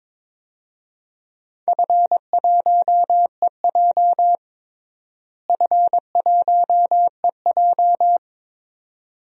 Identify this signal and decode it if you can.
Morse code, heard as a keyed tone, about 22 wpm, 700 Hz. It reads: F1EJ F1EJ
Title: DXCC-100 Call Signs - 4 Characters - Repeated 2x - 22wpm